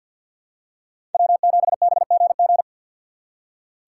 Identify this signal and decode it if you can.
Morse code, heard as a keyed tone, about 46 wpm, 700 Hz. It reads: W7BGG
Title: USA Call Signs - 3x Character Spacing - 50wpm